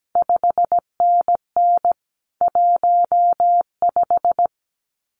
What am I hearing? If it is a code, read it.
Morse code, heard as a keyed tone, about 17 wpm, 700 Hz. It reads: 5NN 15
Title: CQ WW DX Contest Exchange - 17wpm